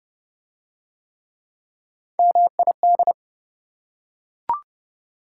Morse code, heard as a keyed tone, about 26 wpm, 700 Hz. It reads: MID